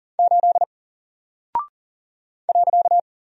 Morse code, heard as a keyed tone, about 40 wpm, 700 Hz. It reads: 8 .
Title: Single Character (Letter:Number:Punctuation:Prosign) 40wpm